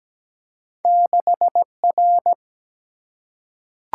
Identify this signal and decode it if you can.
Morse code, heard as a keyed tone, about 17 wpm, 700 Hz. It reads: 6R
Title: Single-Letter Single-Number 17wpm